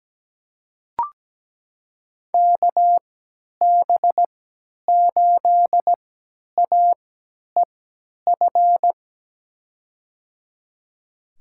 Morse code, heard as a keyed tone, about 17 wpm, spaced as for 10 wpm, 700 Hz. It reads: KB8AEF
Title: USA Call Signs - 3x Character Spacing - 17wpm